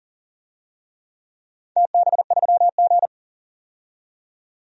Morse code, heard as a keyed tone, about 40 wpm, 700 Hz. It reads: TB3G
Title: DXCC-100 Call Signs - 4 Characters - 40wpm